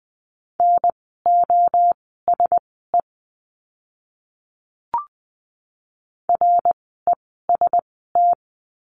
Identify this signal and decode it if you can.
Morse code, heard as a keyed tone, about 20 wpm, spaced as for 15 wpm, 700 Hz. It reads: NOSE REST
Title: 061 Lesson 11f - Letter h - Words 20wpm